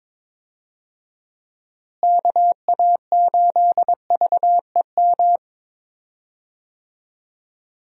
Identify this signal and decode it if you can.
Morse code, heard as a keyed tone, about 22 wpm, 700 Hz. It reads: KA8VEM
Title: USA Call Signs - 6 Characters - Repeated 2x - 22wpm